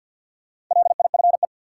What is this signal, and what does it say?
Morse code, heard as a keyed tone, about 52 wpm, 700 Hz. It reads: LIFE
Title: Sentences from Top 300 Words - New Word Drill - 50wpm